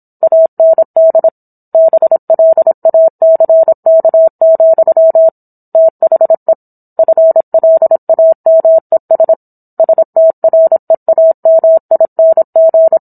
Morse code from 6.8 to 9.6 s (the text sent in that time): FLAMES